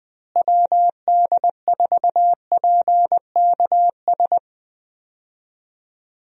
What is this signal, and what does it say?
Morse code, heard as a keyed tone, about 20 wpm, 700 Hz. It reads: WD4PKS
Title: USA Call Signs - 6 Characters - Repeated 3x - 20wpm